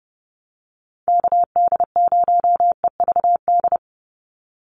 Morse code, heard as a keyed tone, about 30 wpm, 700 Hz. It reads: KD0EVD
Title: USA Call Signs 30wpm